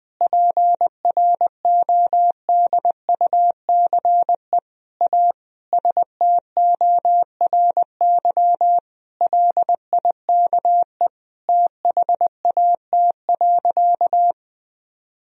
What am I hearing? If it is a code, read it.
Morse code, heard as a keyed tone, about 20 wpm, 700 Hz. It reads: PRODUCE A STORY LIKE THAT.
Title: Sentences from Top 1000 Words - Review Part 3 - 20wpm